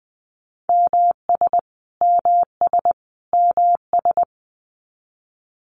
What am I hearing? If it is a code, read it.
Morse code, heard as a keyed tone, about 20 wpm, 700 Hz. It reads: MS MS MS
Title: US States Repeated 3x 20wpm